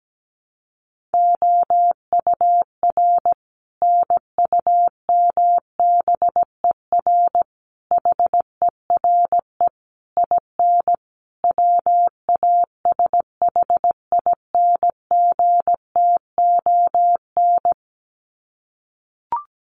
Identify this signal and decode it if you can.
Morse code, heard as a keyed tone, about 17 wpm, 700 Hz. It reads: OUR NUMBER HERE IN WASHINGTON
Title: Sets of 5 Words Encore2 17wpm